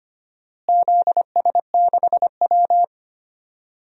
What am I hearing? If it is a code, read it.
Morse code, heard as a keyed tone, about 25 wpm, 700 Hz. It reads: ZS6W
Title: DXCC-100 Call Signs - 4 Characters - Repeated 3x - 25wpm